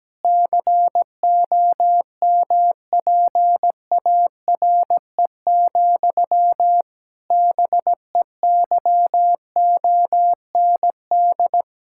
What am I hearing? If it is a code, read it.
Morse code, heard as a keyed tone, about 17 wpm, 700 Hz. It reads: COMPARE, BEYOND
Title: Sentences from Top 1600 Words 17wpm